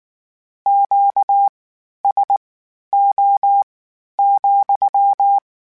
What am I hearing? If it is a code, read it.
Morse code, heard as a keyed tone, about 19 wpm, spaced as for 11 wpm, 800 Hz. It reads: QSO,
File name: Q Calls / Q Calls-mp3s / Q Calls SPSL2 @18